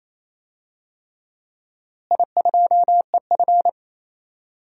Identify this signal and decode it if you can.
Morse code, heard as a keyed tone, about 28 wpm, 700 Hz. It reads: I2EF
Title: DXCC-100 Call Signs - 4 Characters - Repeated 3x - 28wpm